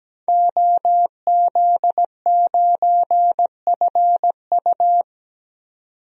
Morse code, heard as a keyed tone, about 17 wpm, 700 Hz. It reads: OZ9FU
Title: DXCC-100 Call Signs - 5 Characters - 17wpm